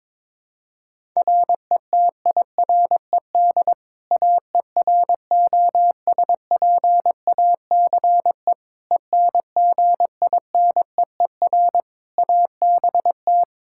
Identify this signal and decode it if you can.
Morse code, heard as a keyed tone, about 21 wpm, 700 Hz. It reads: RETIRED AEROSPACE ENGINEER ABT
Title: K1USN SST QSO Practice 22wpm